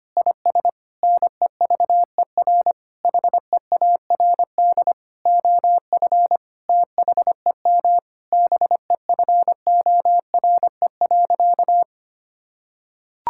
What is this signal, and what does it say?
Morse code, heard as a keyed tone, about 25 wpm, 700 Hz. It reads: IS NEVER HEARD OF THEM BEFORE.
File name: practice / 25wpm / Sentences with 9-Word Phrases - 25wpm